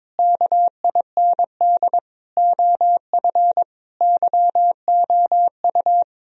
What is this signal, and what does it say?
Morse code, heard as a keyed tone, about 22 wpm, 700 Hz. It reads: KIND OF YOU
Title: Sentences from Top 1000 Words - Review Part 1 - 22wpm